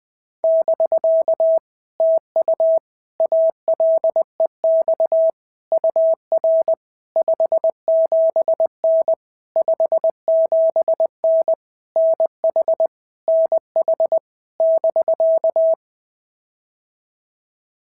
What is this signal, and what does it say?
Morse code, heard as a keyed tone, about 19 wpm, 650 Hz. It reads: <BK> TU ALEX UR 57N 57N NH NH <BK>
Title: POTA QSO Practice - 20wpm